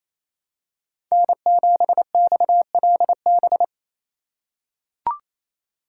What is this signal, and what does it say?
Morse code, heard as a keyed tone, about 28 wpm, 700 Hz. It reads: N7XLB
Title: USA Call Signs - 4x Character Spacing - 28wpm